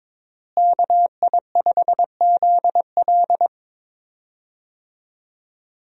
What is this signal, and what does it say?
Morse code, heard as a keyed tone, about 21 wpm, 700 Hz. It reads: KI5ZL
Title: USA Call Signs - 5 Characters - Repeated 2x - 22wpm